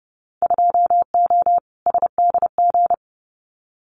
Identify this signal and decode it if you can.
Morse code, heard as a keyed tone, about 30 wpm, 700 Hz. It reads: 2O SDG
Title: Winter Field Day Exchanges 30wpm